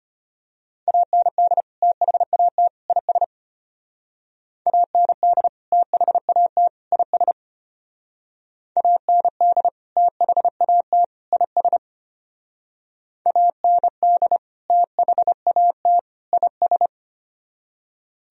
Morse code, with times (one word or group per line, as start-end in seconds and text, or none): 0.9-1.6 AND
1.8-2.7 THAT
2.9-3.2 IS
4.7-5.5 AND
5.7-6.7 THAT
6.9-7.3 IS
8.8-9.7 AND
10.0-11.0 THAT
11.3-11.8 IS
13.3-14.4 AND
14.7-16.0 THAT
16.3-16.9 IS